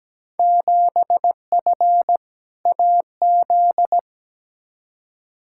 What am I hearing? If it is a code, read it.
Morse code, heard as a keyed tone, about 17 wpm, 700 Hz. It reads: 7F AZ
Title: Field Day Exchanges Encore - 17wpm